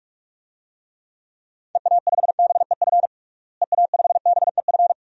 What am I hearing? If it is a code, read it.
Morse code, heard as a keyed tone, about 46 wpm, 700 Hz. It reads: EA5BEF EA5BEF
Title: DXCC-100 Call Signs - 6 Characters - Repeated 2x - 45wpm